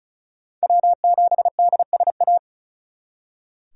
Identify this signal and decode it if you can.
Morse code, heard as a keyed tone, about 45 wpm, 700 Hz. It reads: W7DSA